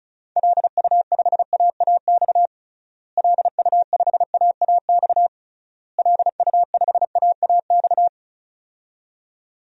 Morse code, heard as a keyed tone, about 36 wpm, 700 Hz. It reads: LU5AAX LU5AAX LU5AAX
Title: DXCC-100 Call Signs - 6 Characters - Repeated 3x - 35wpm